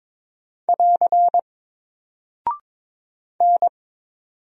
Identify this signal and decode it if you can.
Morse code, heard as a keyed tone, about 22 wpm, 700 Hz. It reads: <AR> N
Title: Single Character (Letter:Number:Punctuation:Prosign) Alt 22wpm